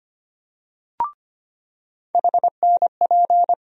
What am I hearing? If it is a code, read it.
Morse code, heard as a keyed tone, about 25 wpm, 700 Hz. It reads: HNP
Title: USA Call Sign Postfixes - 3 Letters 25wpm